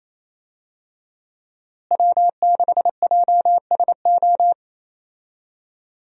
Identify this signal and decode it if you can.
Morse code, heard as a keyed tone, about 28 wpm, 700 Hz. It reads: W6JSO
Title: USA Call Signs - 5 Characters - Repeated 3x - 28wpm